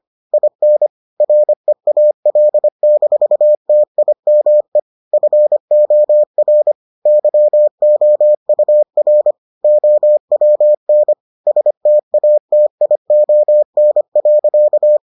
Morse code, heard as a keyed tone, about 25 wpm, 600 Hz. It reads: IN REAL-TIME FOR YOUR OWN STATION.